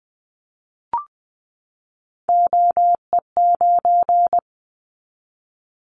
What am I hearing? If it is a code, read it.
Morse code, heard as a keyed tone, about 20 wpm, 700 Hz. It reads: OE9
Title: Callsign Prefixes for Top 50 DXCC - 20wpm